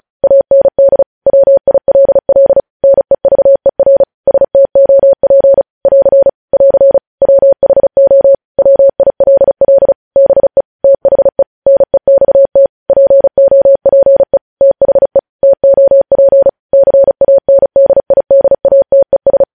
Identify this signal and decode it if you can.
Morse code, heard as a keyed tone, about 36 wpm, 550 Hz. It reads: AND WILL NEVER STOP <AR> <AR> WHO WILL BE THE NEXT POPE THE TOP CANDIDATES
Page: News Headlines in Morse Code at 35 WPM